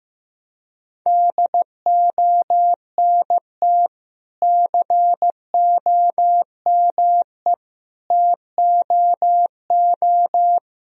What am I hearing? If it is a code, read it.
Morse code, heard as a keyed tone, about 15 wpm, 700 Hz. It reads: DONT COME TOO
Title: Sentences from Top 1000 Words - Review Part 1 - 15wpm